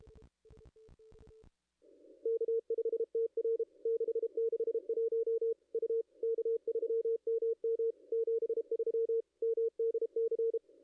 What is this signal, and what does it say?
Morse code, heard as a keyed tone, about 32 wpm, 470 Hz. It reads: K5TR 661 U K3MM 73 MDC
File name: contests / ss-cw-k5tr-op-n5rz -0500